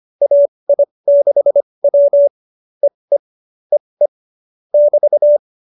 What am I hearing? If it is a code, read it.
Morse code, heard as a keyed tone, about 25 wpm, spaced as for 20 wpm, 570 Hz. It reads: AI6W EE EE <BT>
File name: qso207b 20